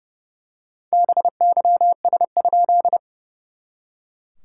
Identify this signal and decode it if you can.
Morse code, heard as a keyed tone, about 30 wpm, 700 Hz. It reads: BYS?
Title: SOTA QSO Practice 30wpm